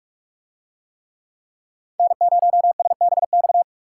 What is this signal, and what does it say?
Morse code, heard as a keyed tone, about 44 wpm, 700 Hz. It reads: N0SBX